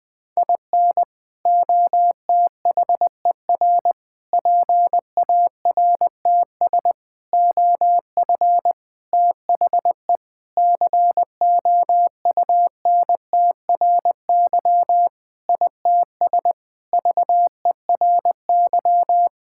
Morse code, heard as a keyed tone, about 20 wpm, 700 Hz. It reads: IN OTHER PARTS OF THE COUNTRY ITS VERY
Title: Sentences with 5-Word Phrases - 20wpm